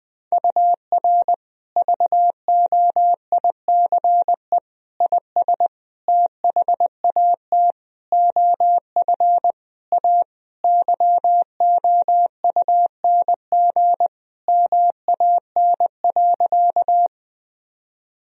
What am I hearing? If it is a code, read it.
Morse code, heard as a keyed tone, about 20 wpm, 700 Hz. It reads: UR VOICE IS THAT OF A YOUNG MAN.
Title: Sentences from Top 500 Words and Abbreviations 20wpm